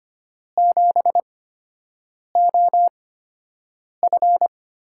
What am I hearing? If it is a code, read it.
Morse code, heard as a keyed tone, about 25 wpm, spaced as for 7 wpm, 700 Hz. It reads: 7OF
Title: USA Call Signs - 4 Characters - 8x Character Spacing - 25wpm